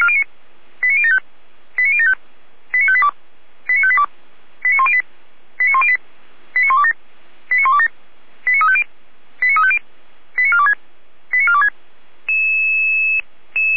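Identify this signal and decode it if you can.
Five-tone selective call (ZVEI), heard as quick runs of five tones, tones of about 70 ms. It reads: 89075 89075 89582 89582 89108 89108 89127 89127 89350 89350 89426 89426
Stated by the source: ZVEI five-tone call received on an SDR.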